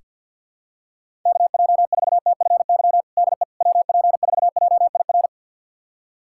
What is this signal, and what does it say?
Morse code, heard as a keyed tone, about 47 wpm, 700 Hz. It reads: KJ4TFY DE WP4JIR
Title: USA Callsign DE Callsign 50wpm